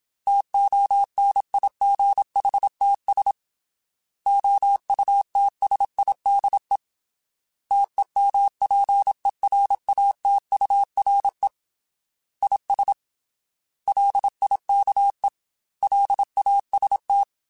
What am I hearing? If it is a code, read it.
Morse code, heard as a keyed tone, about 27 wpm, 800 Hz. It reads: TONIGHTS OUTSIDE TEMPERATURE IS LIKE LAST
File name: Weather Sets / Weather Sets-mp3s / Weather 4 17 @25